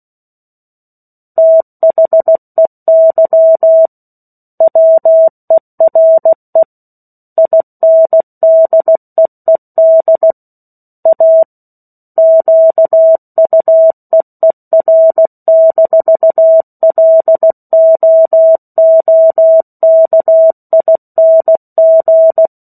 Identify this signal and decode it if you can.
Morse code, heard as a keyed tone, about 16 wpm, 650 Hz. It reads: THEY WERE INDEED A QUEER-LOOKING
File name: 03-Chapter-03 16 WPM